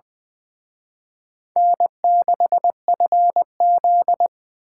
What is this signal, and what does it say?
Morse code, heard as a keyed tone, about 23 wpm, 700 Hz. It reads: N6FZ
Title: Speed Racing - USA 1x2 Call Signs 20wpm